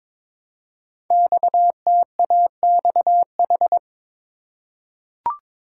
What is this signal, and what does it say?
Morse code, heard as a keyed tone, about 23 wpm, 700 Hz. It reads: XTAXH
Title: Code Groups - 5 Letters - 22wpm